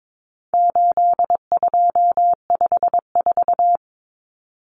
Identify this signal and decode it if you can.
Morse code, heard as a keyed tone, about 22 wpm, 700 Hz. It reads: 8254
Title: Common Cut Numbers - 4-Digit 22wpm